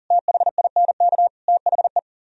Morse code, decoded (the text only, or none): THINK THE